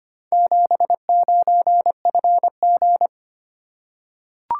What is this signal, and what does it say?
Morse code, heard as a keyed tone, about 25 wpm, 700 Hz. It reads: 79FG
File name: practice / 25wpm / Four Letter-Number Combinations - 25wpm